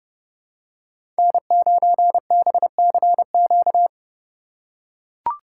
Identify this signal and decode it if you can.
Morse code, heard as a keyed tone, about 30 wpm, 700 Hz. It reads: N9BCQ
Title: USA Call Signs - 5 Characters - 4x Character Spacing - 30wpm